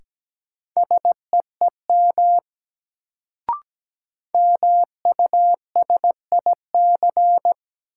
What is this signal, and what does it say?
Morse code, heard as a keyed tone, about 17 wpm, 700 Hz. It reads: SEEM MUSIC
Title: Sentences from Top 300 Words - New Word Drill - 17wpm